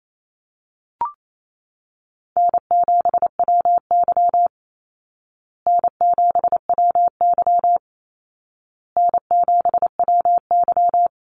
Morse code, heard as a keyed tone, about 28 wpm, 700 Hz. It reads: N7WY N7WY N7WY
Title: Easy Call Signs Repeated 3x - 28wpm